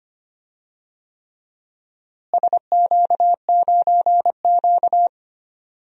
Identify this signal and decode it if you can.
Morse code, heard as a keyed tone, about 25 wpm, 700 Hz. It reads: SQ9Q
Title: DXCC-100 Call Signs - 4 Characters - Repeated 3x - 25wpm